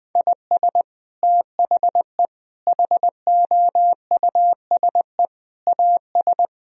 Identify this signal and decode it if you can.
Morse code, heard as a keyed tone, about 19 wpm, 700 Hz. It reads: IS THE HOUSE AS
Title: Sentences from Top 1000 Words - Review Part 4 - 20wpm